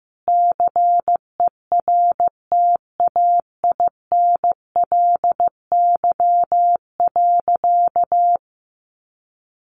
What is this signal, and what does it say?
Morse code, heard as a keyed tone, about 15 wpm, 700 Hz. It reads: CERTAINLY.
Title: Sentences from Top 1500 Words 15wpm